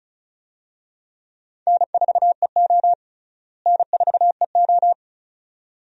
Morse code, heard as a keyed tone, about 36 wpm, 700 Hz. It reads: N4EO N4EO
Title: USA Call Signs - 4 Characters - Repeated 2x - 35wpm